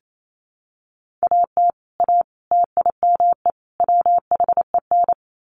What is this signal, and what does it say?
Morse code, heard as a keyed tone, about 28 wpm, 700 Hz. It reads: AT A TIME WHEN
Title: Sets of 4 Words 28wpm